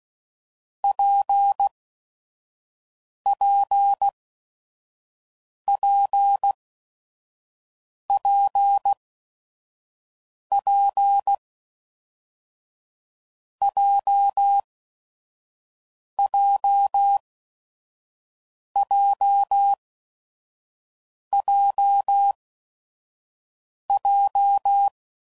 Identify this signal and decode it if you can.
Morse code, heard as a keyed tone, about 16 wpm, spaced as for 5 wpm, 800 Hz. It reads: PPPPP JJJJJ
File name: Combinations / Difficult Combos / WPJ WPJ1